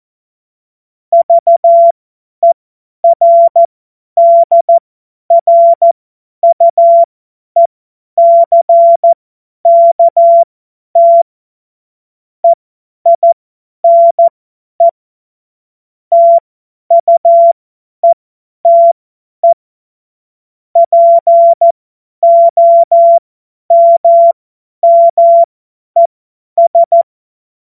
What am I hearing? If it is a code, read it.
Morse code, heard as a keyed tone, about 14 wpm, spaced as for 10 wpm, 680 Hz. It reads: VERDRUECKT EINE TUETE POMMES
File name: Morsekurs / Witze / Witz 10.Oktober 2023 50 F70